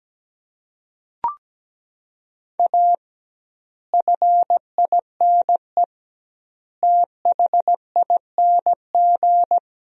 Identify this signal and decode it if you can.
Morse code, heard as a keyed tone, about 17 wpm, 700 Hz. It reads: A FINE THING
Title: 2x Word-Spacing - Sentences from Top 500 Words - 17wpm